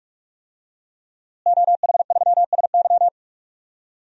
Morse code, heard as a keyed tone, about 47 wpm, 700 Hz. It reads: OH3SY
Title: DXCC-100 Call Signs - 5 Characters - Repeated 2x - 45wpm